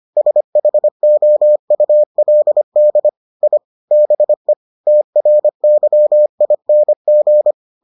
Morse code, heard as a keyed tone, about 25 wpm, 600 Hz. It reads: SHOULD I BE TRYING